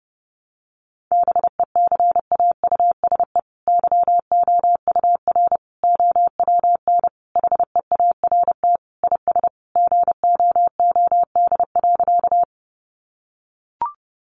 Morse code, heard as a keyed tone, about 30 wpm, 700 Hz. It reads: BECAUSE YOUR OWN HEART IS GOOD.
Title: Sentences from Top 1200 Words 30wpm